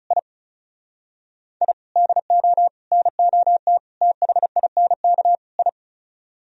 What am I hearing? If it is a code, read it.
Morse code, heard as a keyed tone, about 34 wpm, 700 Hz. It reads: I I DO NOT THINK I
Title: Speed-Racing - Sets of 5 Words 30wpm